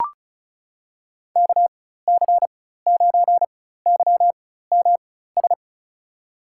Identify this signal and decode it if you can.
Morse code, heard as a keyed tone, about 36 wpm, spaced as for 16 wpm, 700 Hz. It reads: KC9YMS
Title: USA Call Signs - 6 Characters - 4x Character Spacing - 35wpm